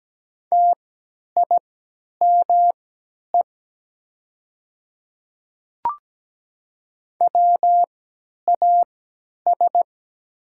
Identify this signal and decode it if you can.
Morse code, heard as a keyed tone, about 17 wpm, spaced as for 10 wpm, 700 Hz. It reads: TIME WAS
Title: 3x Character Spacing - Top 200 Words - 17wpm